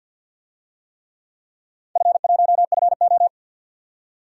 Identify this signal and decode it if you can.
Morse code, heard as a keyed tone, about 53 wpm, 700 Hz. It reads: F1FO
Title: DXCC-100 Call Signs - 4 Characters - 50wpm